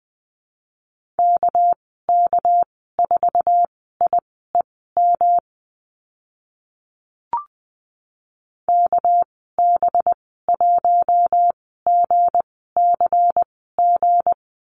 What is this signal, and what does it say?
Morse code, heard as a keyed tone, about 20 wpm, spaced as for 14 wpm, 700 Hz. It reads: KK4IEM KB1GCG
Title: 219 Lesson 35g - Number 0 - Call Signs 20wpm